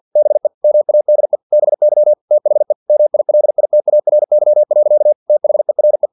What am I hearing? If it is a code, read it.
Morse code, heard as a keyed tone, about 50 wpm, 600 Hz. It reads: BE MADE BY THE MILITARY. THERE
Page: Never 49wpm